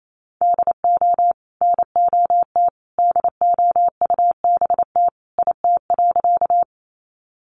Morse code, heard as a keyed tone, about 28 wpm, 700 Hz. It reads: DO NOT DOUBT IT.
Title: Sentences from Top 1500 Words 28wpm